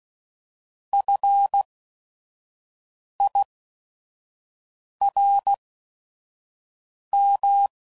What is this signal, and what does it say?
Morse code, heard as a keyed tone, about 16 wpm, spaced as for 5 wpm, 800 Hz. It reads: FIRM